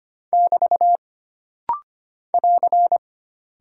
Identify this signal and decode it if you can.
Morse code, heard as a keyed tone, about 25 wpm, 700 Hz. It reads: <BT> <AR>
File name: Single Punctuation-Prosign - Rapid-Fire - 25wpm